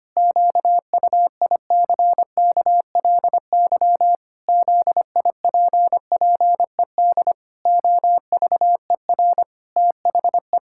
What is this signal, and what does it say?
Morse code, heard as a keyed tone, about 25 wpm, 700 Hz. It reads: QUICKLY ZIPPED OVER THE
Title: Pangram Challenge v2 - 25wpm